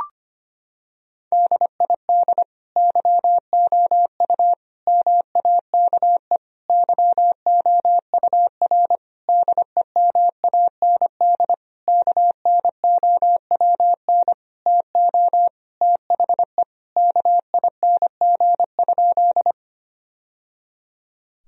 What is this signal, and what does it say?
Morse code, heard as a keyed tone, about 25 wpm, 700 Hz. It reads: DID YOU MAKE YOUR DEMAND KNOWN TO THE KING?
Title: Sentences from Top 1000 Words - Review Part 4 - 25wpm